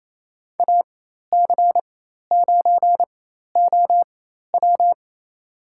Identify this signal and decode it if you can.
Morse code, heard as a keyed tone, about 28 wpm, spaced as for 13 wpm, 700 Hz. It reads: AC9OW
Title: USA Call Signs - 5 Characters - 4x Character Spacing - 28wpm